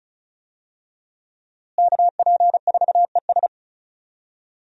Morse code, heard as a keyed tone, about 36 wpm, 700 Hz. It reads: KP4ES
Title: USA Call Signs - 5 Characters - Repeated 3x - 35wpm